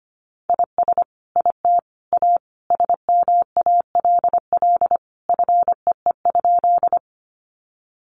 Morse code, heard as a keyed tone, about 25 wpm, 700 Hz. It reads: IS IT A SMALL FEE?
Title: Sentences from Top 1500 Words 25wpm